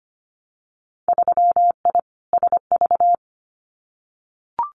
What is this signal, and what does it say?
Morse code, heard as a keyed tone, about 25 wpm, 700 Hz. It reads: 3I SV
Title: Winter Field Day Exchanges Encore 25wpm